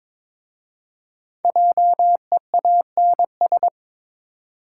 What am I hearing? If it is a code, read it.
Morse code, heard as a keyed tone, about 22 wpm, 700 Hz. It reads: JEANS
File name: practice / 22wpm / Difficult Words with Hints - 22wpm